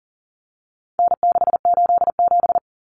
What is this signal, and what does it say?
Morse code, heard as a keyed tone, about 40 wpm, 700 Hz. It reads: N687
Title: Common Cut Numbers - 4-Digit 40wpm